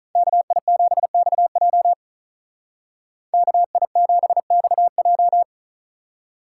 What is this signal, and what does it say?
Morse code, heard as a keyed tone, about 43 wpm, 700 Hz. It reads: KI7XJ KI7XJ
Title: Speed Racing - USA 2x2 Call Signs 35wpm